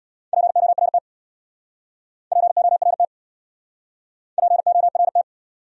Morse code, heard as a keyed tone, about 66 wpm, 700 Hz. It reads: PORT PORT PORT